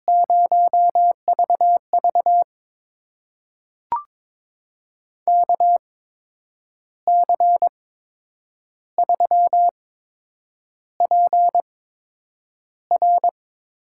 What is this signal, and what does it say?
Morse code, heard as a keyed tone, about 21 wpm, spaced as for 8 wpm, 700 Hz. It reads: 0VV KC3PR
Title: USA Call Signs - 5 Characters - 8x Character Spacing - 22wpm